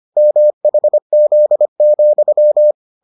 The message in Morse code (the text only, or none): MHZ,